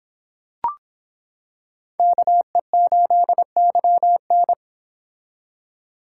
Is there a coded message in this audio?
Yes: Morse code from 2.0 to 4.5 s, heard as a keyed tone, about 26 wpm, 700 Hz. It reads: KE8YN